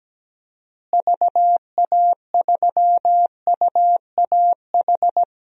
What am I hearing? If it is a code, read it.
Morse code, heard as a keyed tone, about 17 wpm, 700 Hz. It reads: VA3UAH
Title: DXCC-100 Call Signs - 6 Characters - Repeated 3x - 17wpm